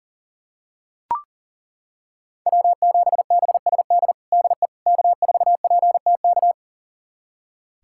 Morse code, heard as a keyed tone, about 37 wpm, 700 Hz. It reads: W7BSD DE K4PTK
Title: USA Callsign DE Callsign - 5 Characters - 40wpm